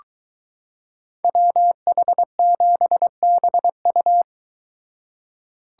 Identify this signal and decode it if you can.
Morse code, heard as a keyed tone, about 23 wpm, 700 Hz. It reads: WH7BU